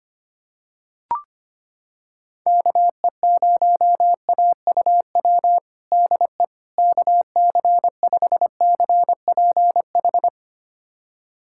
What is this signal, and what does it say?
Morse code, heard as a keyed tone, about 25 wpm, 700 Hz. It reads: KE0AUW DE KC5CPH
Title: USA Callsign DE Callsign - 6 Characters - 25wpm